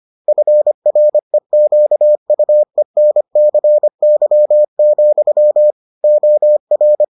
Morse code, heard as a keyed tone, about 25 wpm, 600 Hz. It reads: FREQUENCY, OR